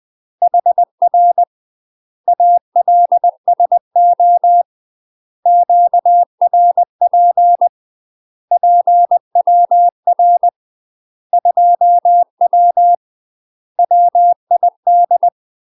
Morse code, heard as a keyed tone, about 20 wpm, 700 Hz. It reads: HR ALSO QRP PWR 2W WID
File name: UK MASTER QSOs TRAINING / 20WPM / UK MASTER QSOs 0052